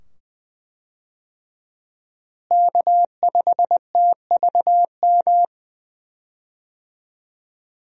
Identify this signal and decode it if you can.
Morse code, heard as a keyed tone, about 20 wpm, 700 Hz. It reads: K5TVM
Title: USA Call Signs - 5 Characters - Repeated 2x - 20wpm